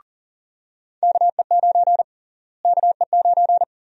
Morse code, heard as a keyed tone, about 40 wpm, 700 Hz. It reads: KE9 KE9
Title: USA Call Sign Prefixes - Repeated 2x - 40wpm